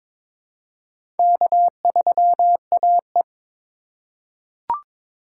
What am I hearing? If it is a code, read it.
Morse code, heard as a keyed tone, about 22 wpm, 700 Hz. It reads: K3AE
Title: USA Call Signs - 4 Characters - 4x Character Spacing - 22wpm